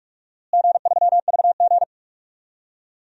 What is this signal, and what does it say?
Morse code, heard as a keyed tone, about 46 wpm, 700 Hz. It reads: G3VG
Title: DXCC-100 Call Signs - 4 Characters - Repeated 3x - 45wpm